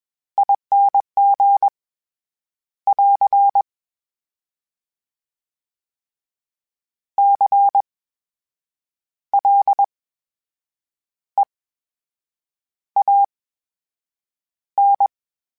Morse code, heard as a keyed tone, about 21 wpm, spaced as for 7 wpm, 800 Hz. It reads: ING <AR> CLEAN
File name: Special INGs-mp3s / Special INGs-mp3s / VerySlowToFaster Ing @20